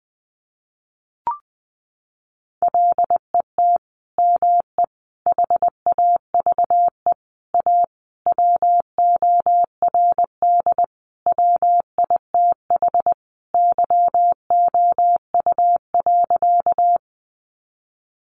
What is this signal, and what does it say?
Morse code, heard as a keyed tone, about 20 wpm, 700 Hz. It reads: LET ME HAVE A WORD WITH YOU.
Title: Sentences from Top 300 Words 20wpm